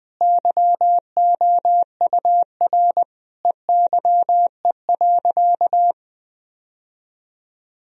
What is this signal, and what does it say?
Morse code, heard as a keyed tone, about 20 wpm, 700 Hz. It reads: YOUR EYE.
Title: Sentences from Top 1000 Words - Review Part 1 - 20wpm